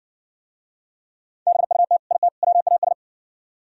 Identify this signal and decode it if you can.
Morse code, heard as a keyed tone, about 60 wpm, 700 Hz. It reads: BUT IT WAS